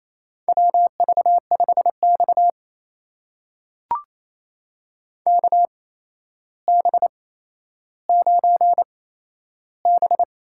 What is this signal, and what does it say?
Morse code, heard as a keyed tone, about 28 wpm, spaced as for 12 wpm, 700 Hz. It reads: WV5X KB9B
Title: USA Call Signs - 8x Character Spacing - 28wpm